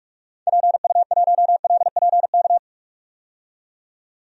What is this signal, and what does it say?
Morse code, heard as a keyed tone, about 46 wpm, 700 Hz. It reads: PU1LPK
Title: DXCC-100 Call Signs - 6 Characters - 45wpm